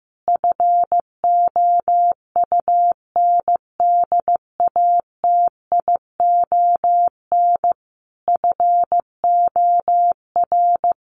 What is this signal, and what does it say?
Morse code, heard as a keyed tone, about 15 wpm, 700 Hz. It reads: FOUNDATION FOR
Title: Sentences from Top 1200 Words 15wpm